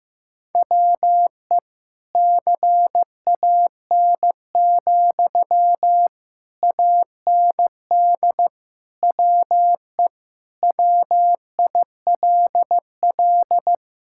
Morse code, heard as a keyed tone, about 16 wpm, 700 Hz. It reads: WE CAN, AND WE WILL
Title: Sentences from Top 1000 Words - Review Part 6 - 15wpm